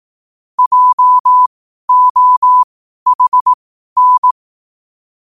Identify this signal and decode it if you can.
Morse code, heard as a keyed tone, about 18 wpm, spaced as for 12 wpm, 1000 Hz. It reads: JOHN
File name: qotdfiles / 12wpm